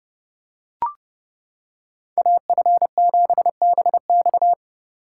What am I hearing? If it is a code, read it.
Morse code, heard as a keyed tone, about 29 wpm, 700 Hz. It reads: AF7BX